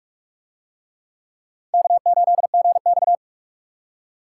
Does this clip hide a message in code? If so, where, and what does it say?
Yes: Morse code from 1.7 to 3.1 s, heard as a keyed tone, about 44 wpm, 700 Hz. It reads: K8GX